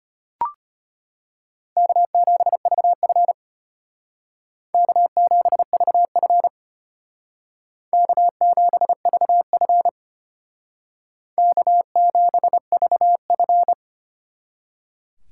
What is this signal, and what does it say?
Morse code, heard as a keyed tone, about 33 wpm, 700 Hz. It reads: K7VF K7VF K7VF K7VF